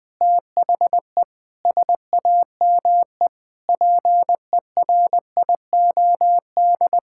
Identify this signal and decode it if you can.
Morse code, heard as a keyed tone, about 20 wpm, 700 Hz. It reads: THE SAME PERIOD
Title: Sentences with 7-Word Phrases - 20wpm